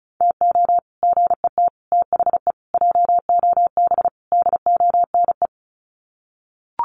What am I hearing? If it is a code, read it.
Morse code, heard as a keyed tone, about 34 wpm, 700 Hz. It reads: TO GET THE JOB DONE
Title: Sets of 5 Words Encore2 35wpm